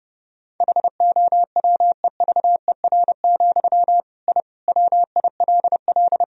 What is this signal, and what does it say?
Morse code, heard as a keyed tone, about 32 wpm, 700 Hz. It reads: HOWEVER, I WILL
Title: Sentences from Top 1000 Words - Review Part 3 - 30wpm